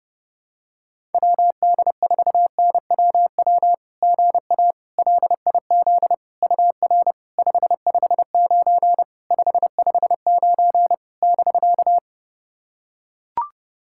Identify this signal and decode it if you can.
Morse code, heard as a keyed tone, about 32 wpm, 700 Hz. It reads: WD4NWW GA LIZ UR 559 559 <BK>
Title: SOTA QSO Practice 30wpm